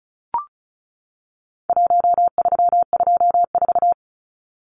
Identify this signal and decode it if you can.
Morse code, heard as a keyed tone, about 34 wpm, 700 Hz. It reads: 1324
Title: Numbers - Contest Style - 4-Digit 35wpm